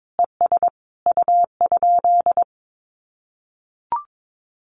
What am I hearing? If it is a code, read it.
Morse code, heard as a keyed tone, about 22 wpm, 700 Hz. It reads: ES U?
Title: Common QSO Phrases Encore 22wpm